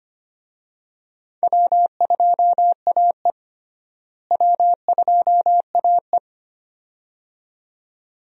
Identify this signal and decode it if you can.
Morse code, heard as a keyed tone, about 25 wpm, 700 Hz. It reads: W2AE W2AE
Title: USA Call Signs - 4 Characters - Repeated 2x - 25wpm